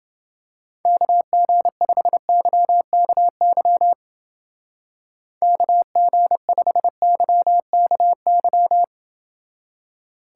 Morse code, heard as a keyed tone, about 28 wpm, 700 Hz. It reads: KG5YKY KG5YKY